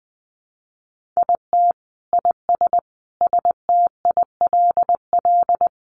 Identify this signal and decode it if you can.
Morse code, heard as a keyed tone, about 20 wpm, 700 Hz. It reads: IT IS STILL
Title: Sentences from Top 1300 Words 20wpm